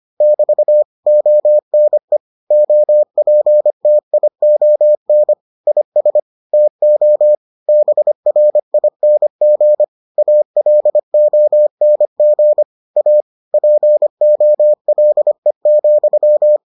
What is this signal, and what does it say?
Morse code, heard as a keyed tone, about 25 wpm, 600 Hz. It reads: <BT> ONE OPTION IS TO BRING ALONG A POLE,